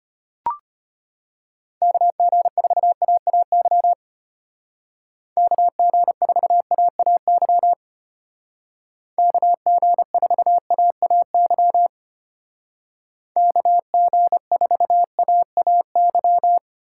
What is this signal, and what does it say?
Morse code, heard as a keyed tone, about 30 wpm, 700 Hz. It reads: KG4AAY KG4AAY KG4AAY KG4AAY
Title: Speed Racing - USA 2x3 Call Signs 25wpm